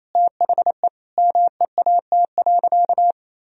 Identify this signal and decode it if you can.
Morse code, heard as a keyed tone, about 27 wpm, 700 Hz. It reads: THE MEAT.
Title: Sentences from Top 1000 Words - Review Part 3 - 28wpm